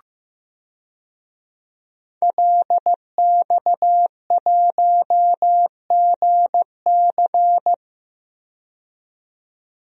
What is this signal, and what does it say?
Morse code, heard as a keyed tone, about 15 wpm, 700 Hz. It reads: LX1GC
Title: DXCC-100 Call Signs - 5 Characters - 15wpm